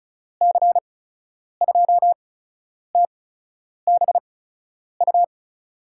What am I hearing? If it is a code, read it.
Morse code, heard as a keyed tone, about 34 wpm, spaced as for 10 wpm, 700 Hz. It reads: C2TBU
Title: USA Call Signs - 6 Characters - 8x Character Spacing - 35wpm